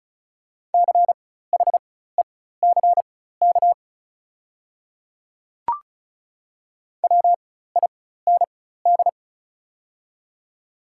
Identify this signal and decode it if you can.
Morse code, heard as a keyed tone, about 34 wpm, spaced as for 16 wpm, 700 Hz. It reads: CHECK WIND
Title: 4x Character Spacing - Top 500 Words - 35wpm